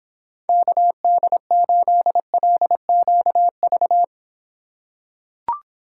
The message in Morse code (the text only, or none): KD8LQV